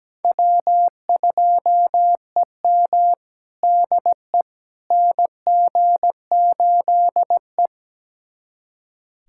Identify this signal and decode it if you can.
Morse code, heard as a keyed tone, about 17 wpm, 700 Hz. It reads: W2EM DE NG8E
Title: USA Callsign DE Callsign - 4 Characters - 17wpm